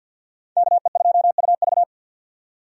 Morse code, heard as a keyed tone, about 53 wpm, 700 Hz. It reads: KE2UV
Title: USA Call Signs - 5 Characters - 50wpm